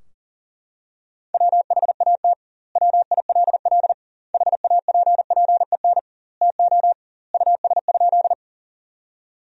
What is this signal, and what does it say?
Morse code, heard as a keyed tone, about 40 wpm, 700 Hz. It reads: WHAT WILL HAPPEN TO US?
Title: 2x Word-Spacing - Sentences from Top 400 Words - 40wpm